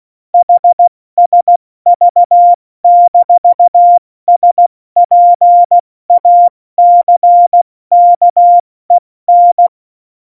Morse code, heard as a keyed tone, about 16 wpm, 700 Hz. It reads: HSV-SPACKEN
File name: Morsekurs / Witze / Witz 05. Januar 2021, 70 F80